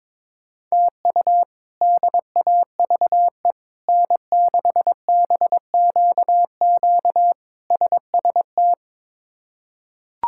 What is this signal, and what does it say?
Morse code, heard as a keyed tone, about 22 wpm, 700 Hz. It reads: TU DAVE N6BQQ SST
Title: K1USN SST QSO Practice - 22wpm